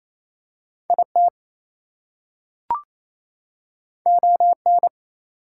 Morse code, heard as a keyed tone, about 27 wpm, 700 Hz. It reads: IT ON
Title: Two Letter Words - 28wpm